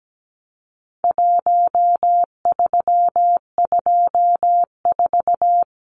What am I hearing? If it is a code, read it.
Morse code, heard as a keyed tone, about 17 wpm, 700 Hz. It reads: 1324
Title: Numbers - Contest Style - 4-Digit 17wpm